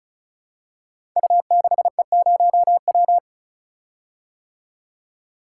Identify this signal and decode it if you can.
Morse code, heard as a keyed tone, about 36 wpm, 700 Hz. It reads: U6E0W
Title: Code Groups - 5 Letter:Numbers - 35wpm